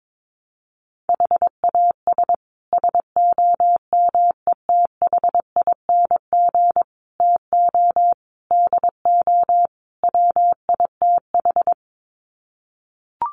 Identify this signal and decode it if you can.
Morse code, heard as a keyed tone, about 22 wpm, 700 Hz. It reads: HAS SOMETHING TO DO WITH
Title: Sets of 5 Words Encore 22wpm